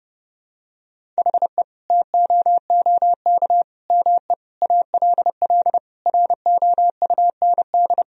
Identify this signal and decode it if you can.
Morse code, heard as a keyed tone, about 30 wpm, 700 Hz. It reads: HE TOOK ME ALL ROUND
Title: Sentences from Top 1000 Words - Review Part 1 - 30wpm